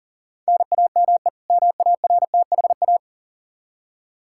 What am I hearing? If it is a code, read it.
Morse code, heard as a keyed tone, about 40 wpm, 700 Hz. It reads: NAME MARTHA
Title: Practice QSO - Exchange 1 - Beginning - 40wpm